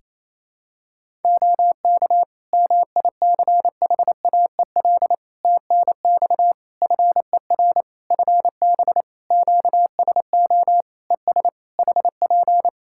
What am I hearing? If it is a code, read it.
Morse code, heard as a keyed tone, about 28 wpm, 700 Hz. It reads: OK MICHAEL TNX FER FB QSO ES HP
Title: Practice QSO - Exchange 4 - Ending - 28wpm